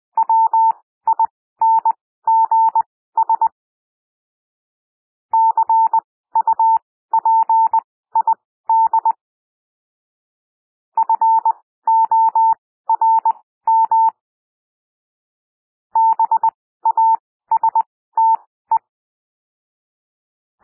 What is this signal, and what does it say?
Morse code, heard as a keyed tone, about 20 wpm, spaced as for 14 wpm, 900 Hz. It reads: WINGS CUPID FORM BASTE
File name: SOUND WORDS 20 wpm